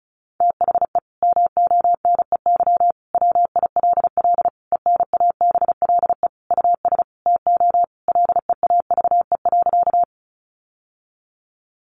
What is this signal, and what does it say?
Morse code, heard as a keyed tone, about 34 wpm, 700 Hz. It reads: THE MONEY WILL ENABLE US TO LEAVE.
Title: Sentences from Top 1500 Words 35wpm